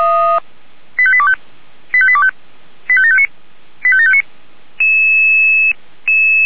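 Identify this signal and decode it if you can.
Five-tone selective call (ZVEI), heard as quick runs of five tones, tones of about 70 ms. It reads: 85825 85825 85759 85759
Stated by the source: Five-tone ZVEI selective call 85730 received by SDR.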